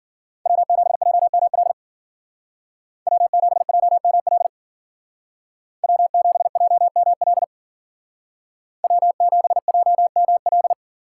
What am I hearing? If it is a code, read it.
Morse code, heard as a keyed tone, about 49 wpm, 700 Hz. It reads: W7JML W7JML W7JML W7JML
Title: Speed Racing - USA 1x3 Call Signs 40wpm